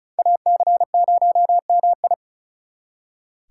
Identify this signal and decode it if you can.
Morse code, heard as a keyed tone, about 44 wpm, 700 Hz. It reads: AC0MI